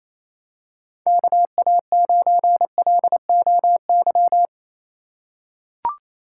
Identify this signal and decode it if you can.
Morse code, heard as a keyed tone, about 28 wpm, 700 Hz. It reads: KA9LOY
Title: USA Call Signs - 6 Characters - 3x Character Spacing - 28wpm